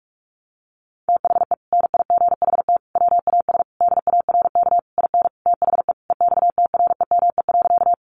Morse code, heard as a keyed tone, about 44 wpm, 700 Hz. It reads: THE NIGHT WAS DARK IN THE EXTREME.
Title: Sentences from Top 1300 Words 45wpm